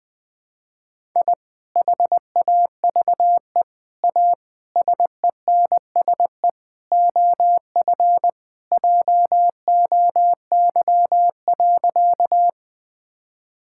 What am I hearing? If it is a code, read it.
Morse code, heard as a keyed tone, about 21 wpm, 700 Hz. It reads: I HAVE A SENSE OF JOY.
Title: Sentences with 3-Word Phrases - 20wpm